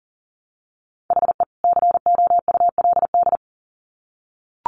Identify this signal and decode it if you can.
Morse code, heard as a keyed tone, about 40 wpm, 700 Hz. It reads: HE COULD
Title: Sets of 2 Words 40wpm